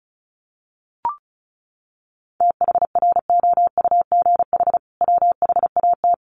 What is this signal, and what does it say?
Morse code, heard as a keyed tone, about 34 wpm, 700 Hz. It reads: THROUGH WHAT
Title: Sentences from Top 900 Words 35wpm